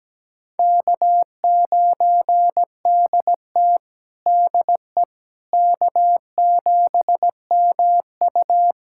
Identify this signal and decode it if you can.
Morse code, heard as a keyed tone, about 17 wpm, 700 Hz. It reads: K9DT DE K7MU
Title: USA Callsign DE Callsign - 4 Characters - 17wpm